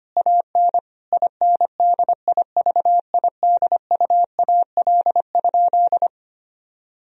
Morse code, heard as a keyed tone, about 25 wpm, 700 Hz. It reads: AN INDIVIDUAL?
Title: Sentences from Top 1000 Words - Review Part 4 - 25wpm